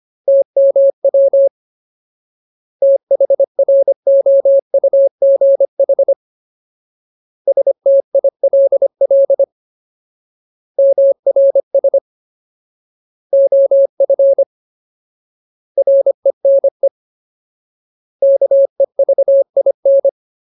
Morse code, heard as a keyed tone, about 25 wpm, 550 Hz. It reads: TMW THROUGH STILL MRS OF RENE KEVIN
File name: Ultimate Word List / 25WPM Ultimate